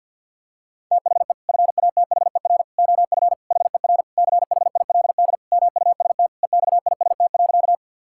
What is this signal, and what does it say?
Morse code, heard as a keyed tone, about 52 wpm, 700 Hz. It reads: THE FATHER OF HER CHILD MUST EXIST.
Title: Sentences from Top 1000 Words - Review Part 5 - 50wpm